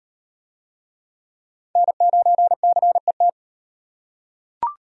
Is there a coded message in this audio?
Yes: Morse code from 1.8 to 3.3 s, heard as a keyed tone, about 38 wpm, 700 Hz. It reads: N9CET